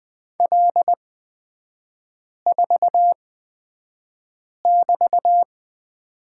Morse code, heard as a keyed tone, about 20 wpm, spaced as for 5 wpm, 700 Hz. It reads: L4<BT>
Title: 224 Lesson 36e - BT - Three Characters 20wpm